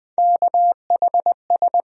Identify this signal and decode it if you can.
Morse code, heard as a keyed tone, about 20 wpm, 700 Hz. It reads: KHS
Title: USA Call Sign Postfixes - 3 Letters 20wpm